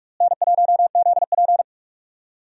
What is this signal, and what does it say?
Morse code, heard as a keyed tone, about 46 wpm, 700 Hz. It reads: N1ZP
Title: USA Call Signs - 4 Characters - Repeated 3x - 45wpm